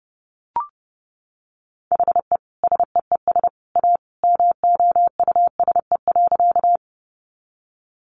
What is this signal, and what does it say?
Morse code, heard as a keyed tone, about 30 wpm, 700 Hz. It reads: HE SEES A MOUSE.
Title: Easy Phrases 30wpm